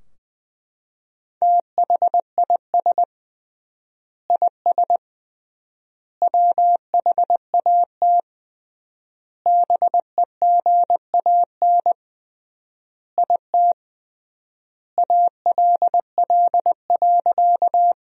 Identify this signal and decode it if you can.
Morse code, heard as a keyed tone, about 20 wpm, 700 Hz. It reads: THIS IS WHAT BEGAN IT ALL.
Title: 3x Word-Spacing - Sentences from Top 300 Words - 20wpm